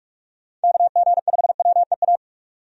Morse code, heard as a keyed tone, about 46 wpm, 700 Hz. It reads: KG5WEA
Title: USA Call Signs - 6 Characters - 45wpm